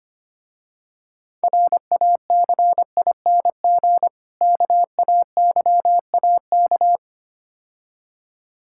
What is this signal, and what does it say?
Morse code, heard as a keyed tone, about 25 wpm, 700 Hz. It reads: RACING KAYAK
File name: practice / 25wpm / Difficult Words - Speed-Racing with Hints - 25wpm